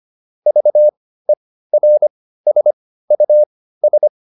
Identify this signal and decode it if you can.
Morse code, heard as a keyed tone, about 25 wpm, spaced as for 15 wpm, 600 Hz. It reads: VERSUS